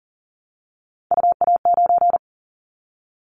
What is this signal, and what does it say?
Morse code, heard as a keyed tone, about 40 wpm, 700 Hz. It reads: UA9
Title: Callsign Prefixes for Top 100 DXCC - 40wpm